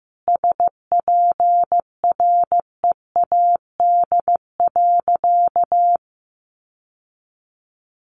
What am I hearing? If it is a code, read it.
Morse code, heard as a keyed tone, about 15 wpm, 700 Hz. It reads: SPREAD.
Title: Sentences from Top 1200 Words 15wpm